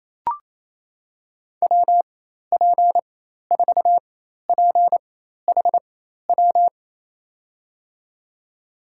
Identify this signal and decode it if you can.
Morse code, heard as a keyed tone, about 28 wpm, spaced as for 13 wpm, 700 Hz. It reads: WP4PHW
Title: USA Call Signs - 4x Character Spacing - 28wpm